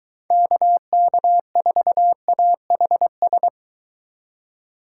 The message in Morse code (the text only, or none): KK4AHS